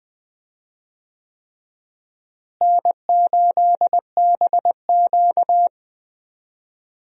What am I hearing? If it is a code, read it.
Morse code, heard as a keyed tone, about 20 wpm, 700 Hz. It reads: N8BQ
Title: USA Call Signs - 4 Characters - Repeated 3x - 20wpm